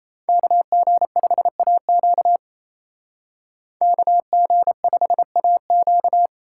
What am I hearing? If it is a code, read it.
Morse code, heard as a keyed tone, about 31 wpm, 700 Hz. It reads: KG5AQ KG5AQ
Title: Speed Racing - USA 2x2 Call Signs 28wpm